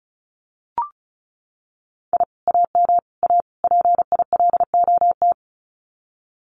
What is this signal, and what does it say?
Morse code, heard as a keyed tone, about 34 wpm, 700 Hz. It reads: I AM A PILOT
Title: Common QSO Phrases Encore 35wpm